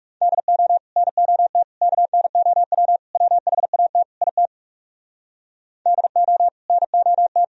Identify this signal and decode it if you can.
Morse code, heard as a keyed tone, about 39 wpm, 700 Hz. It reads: DO NOT KNOW WHAT IT DO NOT